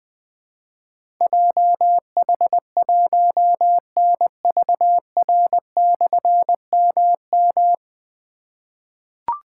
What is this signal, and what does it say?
Morse code, heard as a keyed tone, about 21 wpm, 700 Hz. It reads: JH1NVR/MM
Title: Difficult Call Signs 20wpm